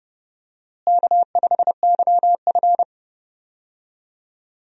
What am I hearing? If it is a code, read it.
Morse code, heard as a keyed tone, about 30 wpm, 700 Hz. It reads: K5YF
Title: USA Call Signs - 4 Characters - Repeated 2x - 30wpm